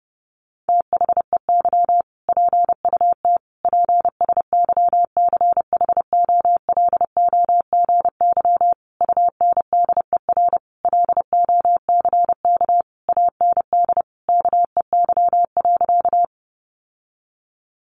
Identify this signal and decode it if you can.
Morse code, heard as a keyed tone, about 30 wpm, 700 Hz. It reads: THEY PUT PSYCHOLOGY UNDER LOCK AND KEY.
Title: Sentences from Top 1400 Words 30wpm